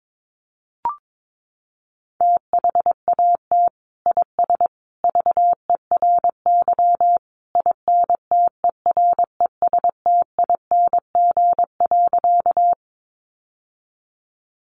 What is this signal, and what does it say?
Morse code, heard as a keyed tone, about 22 wpm, 700 Hz. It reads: THAT IS VERY INTERESTING.
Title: Sentences from Top 1000 Words 22wpm